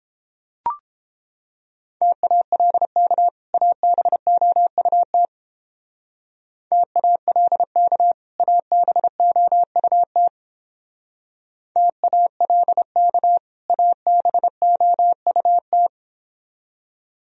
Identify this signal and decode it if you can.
Morse code, heard as a keyed tone, about 29 wpm, 700 Hz. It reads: TALK ABOUT TALK ABOUT TALK ABOUT